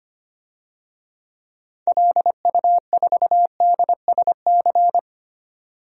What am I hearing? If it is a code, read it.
Morse code, heard as a keyed tone, about 25 wpm, 700 Hz. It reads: LU4DSC
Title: DXCC-100 Call Signs - 6 Characters - 25wpm